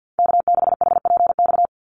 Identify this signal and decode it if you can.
Morse code, heard as a keyed tone, about 50 wpm, 700 Hz. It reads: K6HPX
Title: NA SOTA Activator Callsigns 50wpm